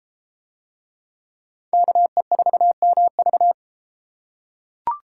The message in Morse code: KE4MV

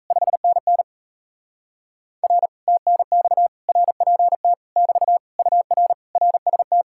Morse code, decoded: 5NN R TNX RPT <BT> UR RST

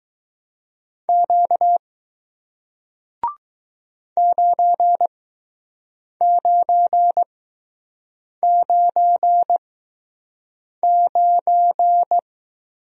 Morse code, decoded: Q 9999

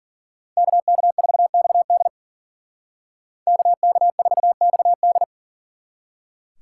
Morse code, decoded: KK4XD KK4XD

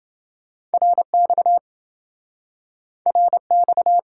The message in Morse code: RX RX

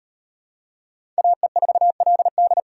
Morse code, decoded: AE4LD

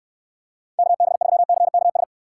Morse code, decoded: KB3CKS

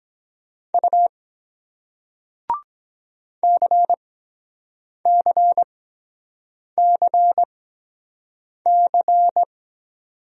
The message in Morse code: U CCCC

